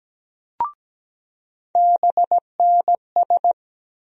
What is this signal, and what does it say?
Morse code, heard as a keyed tone, about 17 wpm, 700 Hz. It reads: BNS
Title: USA Call Sign Postfixes - 3 Letters 17wpm